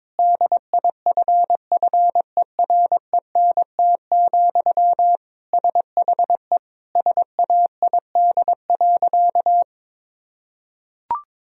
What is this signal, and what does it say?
Morse code, heard as a keyed tone, about 22 wpm, 700 Hz. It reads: DIFFERENT, SHE SAID.
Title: Sentences from Top 1000 Words - Review Part 3 - 22wpm